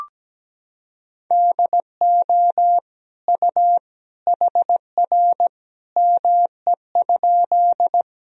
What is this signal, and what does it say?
Morse code, heard as a keyed tone, about 17 wpm, 700 Hz. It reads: DO U HR ME?
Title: Sentences from Top 300 Words and Abbreviations 17wpm